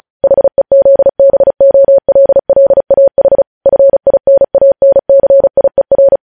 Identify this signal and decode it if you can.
Morse code, heard as a keyed tone, about 36 wpm, 550 Hz. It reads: HEZBOLLAH FINANCIER
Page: News Headlines in Morse Code at 35 WPM